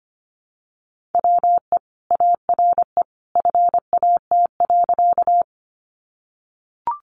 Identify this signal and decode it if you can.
Morse code, heard as a keyed tone, about 25 wpm, 700 Hz. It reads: WE ARE FAT.
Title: Sentences from Top 600 Words 25wpm